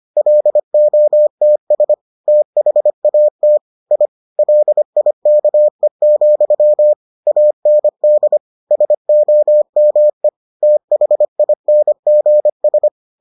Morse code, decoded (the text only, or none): LOTS THAT I LIKE, AND SOME THINGS